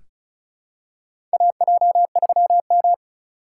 Morse code, heard as a keyed tone, about 36 wpm, 700 Hz. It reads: AJ3M
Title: USA Call Signs - 4 Characters - 4x Character Spacing - 35wpm